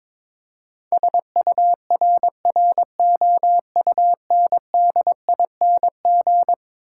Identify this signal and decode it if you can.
Morse code, heard as a keyed tone, about 22 wpm, 700 Hz. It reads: SURROUNDING
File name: practice / 22wpm / Eleven Letter Words - 22wpm